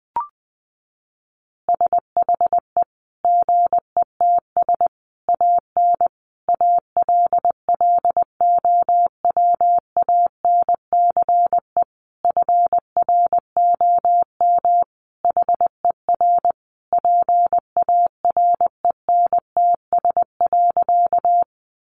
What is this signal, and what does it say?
Morse code, heard as a keyed tone, about 20 wpm, 700 Hz. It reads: SHE GETS AN ALLOWANCE FROM HER PARENTS.
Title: Easy Phrases 20wpm